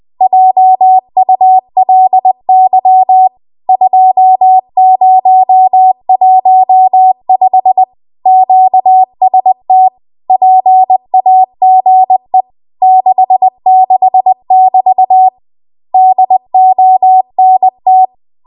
Morse code, heard as a keyed tone, about 20 wpm, 750 Hz. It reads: JULY 2015 QST PAGE 66<BT> DONT
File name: Archive / 20 WPM / 20WPM